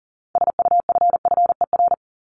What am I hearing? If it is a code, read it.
Morse code, heard as a keyed tone, about 40 wpm, 700 Hz. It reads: SUFFER
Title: Top 1501-1750 Words 40wpm